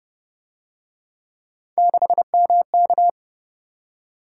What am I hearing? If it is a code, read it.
Morse code, heard as a keyed tone, about 31 wpm, 700 Hz. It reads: 6MK
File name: Speed-Racing - Three Letter-Number Combinations - 25wpm